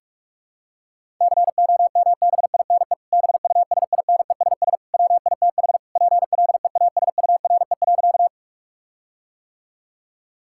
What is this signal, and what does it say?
Morse code, heard as a keyed tone, about 46 wpm, 700 Hz. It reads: COMBINE BUSINESS WITH PLEASURE.
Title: Sentences from Top 1600 Words 45wpm